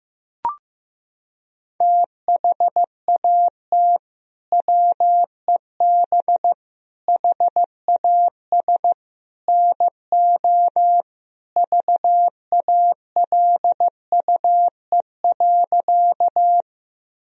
Morse code, heard as a keyed tone, about 15 wpm, 700 Hz. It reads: THAT WEB HAS NO VALUE.
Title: Sentences from Top 1000 Words - Review Part 4 - 15wpm